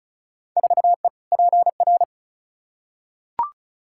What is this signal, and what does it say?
Morse code, heard as a keyed tone, about 36 wpm, 700 Hz. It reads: 4E PR
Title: Field Day Exchanges Encore - 35wpm